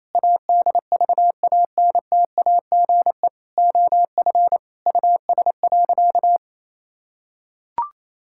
Morse code, heard as a keyed tone, about 28 wpm, 700 Hz. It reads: ADVANTAGE OF US.
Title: Sentences from Top 1000 Words - Review Part 6 - 28wpm